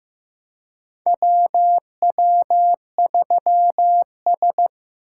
Morse code, heard as a keyed tone, about 15 wpm, 700 Hz. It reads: WW3S
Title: USA Call Signs - 4 Characters - 2x Character Spacing - 15wpm